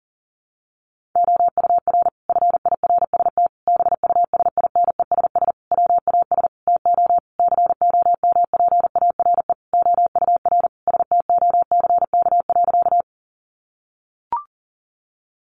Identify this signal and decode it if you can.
Morse code, heard as a keyed tone, about 40 wpm, 700 Hz. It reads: OUR FIRST BUSINESS WAS TO COMPARE OUR STOCK.
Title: Sentences from Top 1300 Words 40wpm